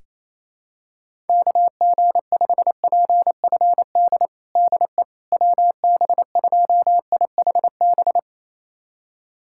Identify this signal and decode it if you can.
Morse code, heard as a keyed tone, about 28 wpm, 700 Hz. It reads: KG5PFD DE WB2IHB
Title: USA Callsign DE Callsign 28wpm